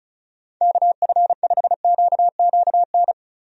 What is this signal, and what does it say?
Morse code, heard as a keyed tone, about 36 wpm, 700 Hz. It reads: KF5QQN
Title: USA Call Signs - 6 Characters - 35wpm